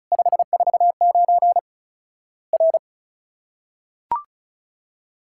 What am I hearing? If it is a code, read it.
Morse code, heard as a keyed tone, about 36 wpm, 700 Hz. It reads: 549 R
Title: Worked All Europe DX Contest - QTC Exchange - 35wpm